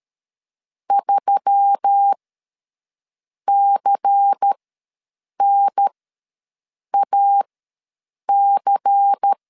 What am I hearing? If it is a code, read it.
Morse code, heard as a keyed tone, about 13 wpm, spaced as for 6 wpm, 800 Hz. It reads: 3CNAC
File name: LEZIONI / 5 LBVC / 5 LBVC ws 12wpm 3159